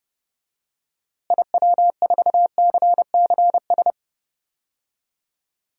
Morse code, heard as a keyed tone, about 30 wpm, 700 Hz. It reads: IW4CCS
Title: DXCC-100 Call Signs - 6 Characters - Repeated 3x - 30wpm